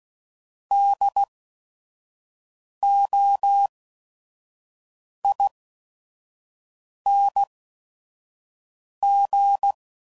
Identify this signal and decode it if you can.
Morse code, heard as a keyed tone, about 16 wpm, spaced as for 5 wpm, 800 Hz. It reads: DOING
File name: Suffixes / Special ING's / VerySlowToFaster ING's